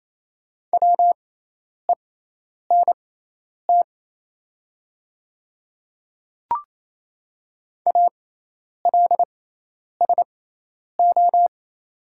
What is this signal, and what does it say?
Morse code, heard as a keyed tone, about 27 wpm, spaced as for 10 wpm, 700 Hz. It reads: WENT ALSO
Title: 6x Character Spacing - Top 200 Words - 28wpm